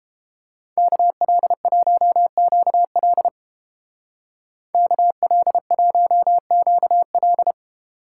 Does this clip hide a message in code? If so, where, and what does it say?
Yes: Morse code from 0.8 to 7.5 s, heard as a keyed tone, about 29 wpm, 700 Hz. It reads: KL1QL KL1QL